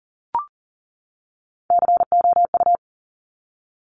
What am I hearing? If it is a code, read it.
Morse code, heard as a keyed tone, about 38 wpm, 700 Hz. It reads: COU
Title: Top Three Letter Combinations in Words 40wpm